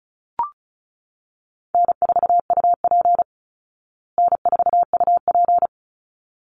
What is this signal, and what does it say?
Morse code, heard as a keyed tone, about 34 wpm, 700 Hz. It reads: N4UP N4UP
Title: Call Signs Repeated 3x - 35wpm